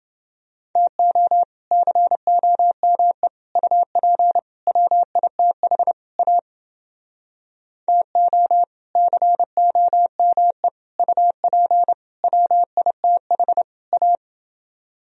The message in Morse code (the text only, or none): TO COME UP WITH A TO COME UP WITH A